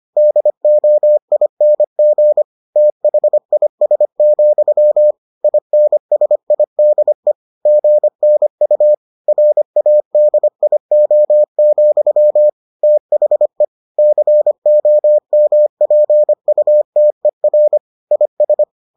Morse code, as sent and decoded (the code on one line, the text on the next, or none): -.. --- .. -. --. / - .... .. ... --..-- / .. -. ... .. -.. / --. -. ..- / .-. .- -.. .. --- --..-- / - .... / -.-. --- -- .--. ..- - . .-. / .. ...
DOING THIS, INSIDE GNU RADIO, THE COMPUTER IS